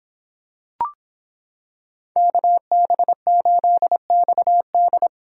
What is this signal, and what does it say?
Morse code, heard as a keyed tone, about 26 wpm, 700 Hz. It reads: KB8XD